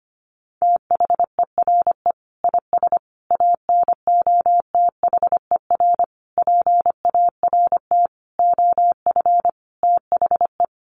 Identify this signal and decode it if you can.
Morse code, heard as a keyed tone, about 25 wpm, 700 Hz. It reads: THERE IS ANOTHER PART OF THE
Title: Sentences from Top 1300 Words 25wpm